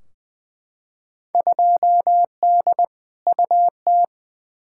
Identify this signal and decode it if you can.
Morse code, heard as a keyed tone, about 21 wpm, 700 Hz. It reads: 2D UT
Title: Field Day Exchanges Encore - 20wpm